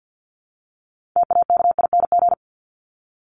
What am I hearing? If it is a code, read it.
Morse code, heard as a keyed tone, about 48 wpm, 700 Hz. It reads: TAKING
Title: ING Suffix - Top 100 Words with Base Words 50wpm